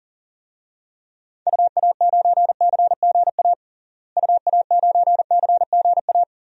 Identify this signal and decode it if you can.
Morse code, heard as a keyed tone, about 40 wpm, 700 Hz. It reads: UA9CGA UA9CGA
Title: DXCC-100 Call Signs - 6 Characters - Repeated 3x - 40wpm